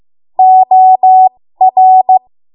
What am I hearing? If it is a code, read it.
Morse code, heard as a keyed tone, about 15 wpm, spaced as for 13 wpm, 750 Hz. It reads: OR
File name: Morse / 13WPM